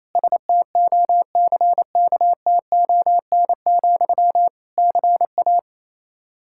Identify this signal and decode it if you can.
Morse code, heard as a keyed tone, about 28 wpm, 700 Hz. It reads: STOCKTON, CA
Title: Practice QSO - Exchange 1 - Beginning - 28wpm